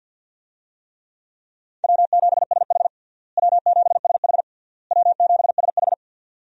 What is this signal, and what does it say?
Morse code, heard as a keyed tone, about 47 wpm, 700 Hz. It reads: W7SH W7SH W7SH
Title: USA Call Signs - 4 Characters - Repeated 3x - 50wpm